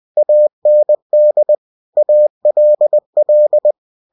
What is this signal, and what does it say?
Morse code, heard as a keyed tone, about 20 wpm, 600 Hz. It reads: AND ALL
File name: cw-books / winnie-the-pooh / winnie-20wpm-page-0103